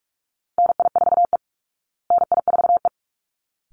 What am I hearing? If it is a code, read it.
Morse code, heard as a keyed tone, about 44 wpm, 700 Hz. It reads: NI4E NI4E
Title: Call Signs Repeated 3x - 45wpm